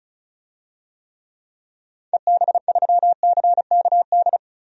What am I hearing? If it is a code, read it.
Morse code, heard as a keyed tone, about 34 wpm, 700 Hz. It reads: EB3CKD
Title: DXCC-100 Call Signs - 6 Characters - 35wpm